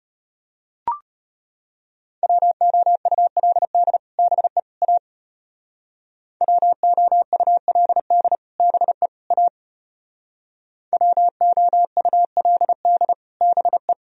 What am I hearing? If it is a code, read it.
Morse code, heard as a keyed tone, about 32 wpm, 700 Hz. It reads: WOULD BE A WOULD BE A WOULD BE